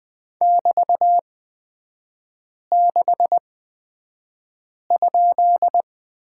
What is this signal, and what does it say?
Morse code, heard as a keyed tone, about 20 wpm, spaced as for 5 wpm, 690 Hz. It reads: <BT>6?
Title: 229 Lesson 37e - Letter x - Three Characters 20wpm